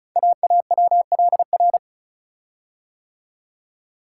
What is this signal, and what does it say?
Morse code, heard as a keyed tone, about 36 wpm, 700 Hz. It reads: AAWLR
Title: Code Groups - 5 Letters - 35wpm